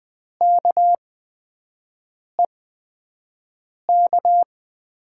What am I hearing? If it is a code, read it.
Morse code, heard as a keyed tone, about 20 wpm, spaced as for 5 wpm, 700 Hz. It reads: KEK
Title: USA Call Signs - 5 Characters - 8x Character Spacing - 20wpm